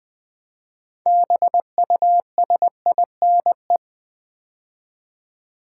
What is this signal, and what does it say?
Morse code, heard as a keyed tone, about 21 wpm, 700 Hz. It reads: BUSINE
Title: Build A Word - 3 Letters At A Time - 8-Letter Words - 20wpm